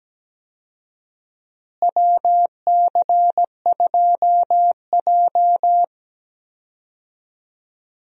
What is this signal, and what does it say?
Morse code, heard as a keyed tone, about 17 wpm, 700 Hz. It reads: WC2J
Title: USA Call Signs - 4 Characters - Repeated 3x - 17wpm